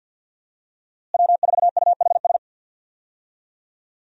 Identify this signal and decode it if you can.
Morse code, heard as a keyed tone, about 50 wpm, 700 Hz. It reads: W4UHS
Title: USA Call Signs - 5 Characters - Repeated 2x - 50wpm